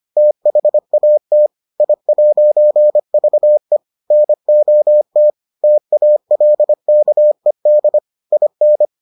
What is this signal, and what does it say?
Morse code, heard as a keyed tone, about 25 wpm, 600 Hz. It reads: THAT I'VE NOT TALKED IN